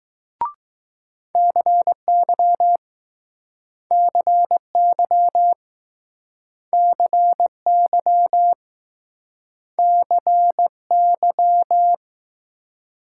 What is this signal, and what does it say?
Morse code, heard as a keyed tone, about 20 wpm, 700 Hz. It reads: CY CY CY CY